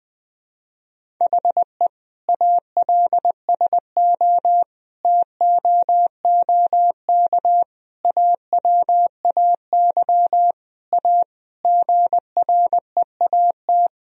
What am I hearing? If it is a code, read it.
Morse code, heard as a keyed tone, about 21 wpm, 700 Hz. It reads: HE ALSO TOOK AWAY A GREAT
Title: Sentences from Top 1000 Words - Review Part 1 - 20wpm